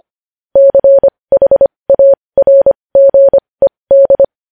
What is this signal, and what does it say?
Morse code, heard as a keyed tone, about 25 wpm, spaced as for 20 wpm, 550 Hz. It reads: CHARGED
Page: News Headlines in Morse Code at 20 WPM